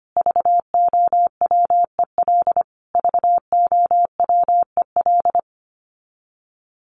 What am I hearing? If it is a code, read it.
Morse code, heard as a keyed tone, about 25 wpm, 700 Hz. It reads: VOWEL VOWEL
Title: Top 500 Words Repeated 25wpm